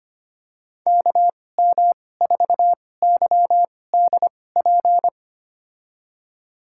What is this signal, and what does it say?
Morse code, heard as a keyed tone, about 25 wpm, spaced as for 18 wpm, 700 Hz. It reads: KM4YDP
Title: USA Call Signs - 6 Characters - 2x Character Spacing - 25wpm